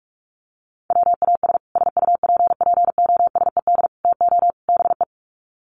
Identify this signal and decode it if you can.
Morse code, heard as a keyed tone, about 44 wpm, 700 Hz. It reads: WAS SUPPOSED TO BE
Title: Sets of 4 Words 45wpm